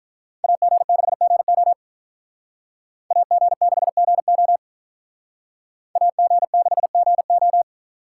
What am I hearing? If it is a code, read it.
Morse code, heard as a keyed tone, about 45 wpm, 700 Hz. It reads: AG6GO AG6GO AG6GO